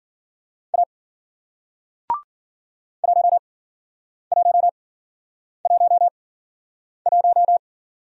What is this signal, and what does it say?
Morse code, heard as a keyed tone, about 54 wpm, spaced as for 9 wpm, 700 Hz. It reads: A 1111